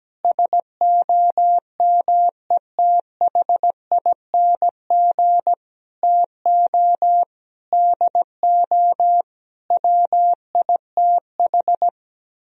Morse code, heard as a keyed tone, about 17 wpm, 700 Hz. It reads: SOMETHING TO DO WITH
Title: Speed-Racing - Sets of 5 Words 17wpm